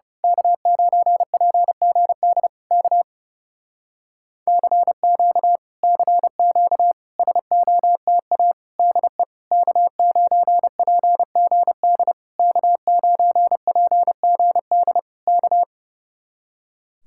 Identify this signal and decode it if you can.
Morse code, heard as a keyed tone, about 28 wpm, 700 Hz. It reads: K9PGD K CQ CQ SOTA DE K9PGD K9PGD K
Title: Speed-Racing Common QSO Phrases Encore - 30wpm